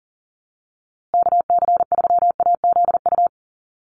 Morse code, heard as a keyed tone, about 40 wpm, 700 Hz. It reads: KC3AZU
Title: USA Call Signs Repeated 2x 40wpm